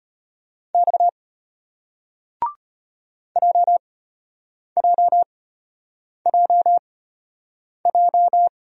Morse code, heard as a keyed tone, about 31 wpm, spaced as for 8 wpm, 700 Hz. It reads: X JJJJ
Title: Speed-Racing Single Letters 25wpm